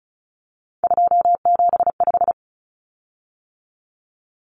Morse code, heard as a keyed tone, about 34 wpm, 700 Hz. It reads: 275
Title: Common Cut Numbers - 3-Digit 35wpm